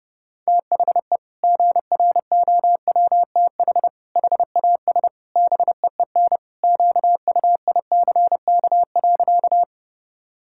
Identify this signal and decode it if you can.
Morse code, heard as a keyed tone, about 30 wpm, 700 Hz. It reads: THE GROWTH HAS BEEN QUICK.
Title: Sentences from Top 1000 Words - Review Part 5 - 30wpm